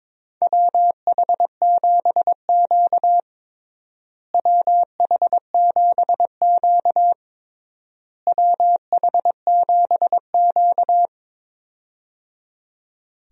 Morse code, heard as a keyed tone, about 22 wpm, 700 Hz. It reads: WH7Q WH7Q WH7Q
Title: USA Call Signs - 4 Characters - Repeated 3x - 22wpm